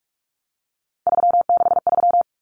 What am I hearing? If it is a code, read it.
Morse code, heard as a keyed tone, about 44 wpm, 700 Hz. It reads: 363
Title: Common Cut Numbers - 3-Digit 45wpm